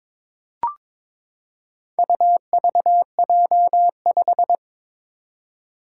Morse code, heard as a keyed tone, about 23 wpm, 700 Hz. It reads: UVJ5
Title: Four Challenging Letter-Number Combinations 22wpm